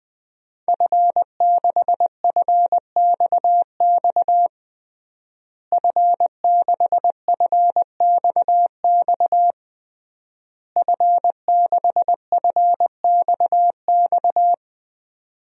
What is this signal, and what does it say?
Morse code, heard as a keyed tone, about 20 wpm, 700 Hz. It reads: F6FXX F6FXX F6FXX
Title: DXCC-100 Call Signs - 5 Characters - Repeated 3x - 20wpm